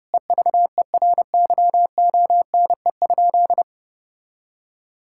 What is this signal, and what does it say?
Morse code, heard as a keyed tone, about 30 wpm, 700 Hz. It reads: EVERYONE?
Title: Sentences from Top 1000 Words - Review Part 5 - 30wpm